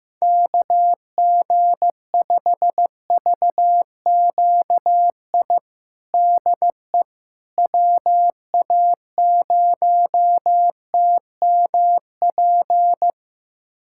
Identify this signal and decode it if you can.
Morse code, heard as a keyed tone, about 15 wpm, 700 Hz. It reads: KG5VQI DE WA0TMP
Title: USA Callsign DE Callsign 15wpm